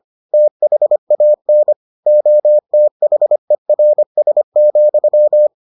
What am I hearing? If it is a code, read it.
Morse code, heard as a keyed tone, about 25 wpm, 600 Hz. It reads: THAN OTHERS,